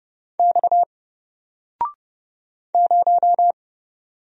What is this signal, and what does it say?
Morse code, heard as a keyed tone, about 30 wpm, 700 Hz. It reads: X 0
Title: Single Letter-Number 30wpm